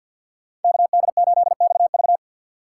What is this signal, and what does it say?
Morse code, heard as a keyed tone, about 47 wpm, 700 Hz. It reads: KD8XV
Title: USA Call Signs - 5 Characters - 2x Character Spacing - 50wpm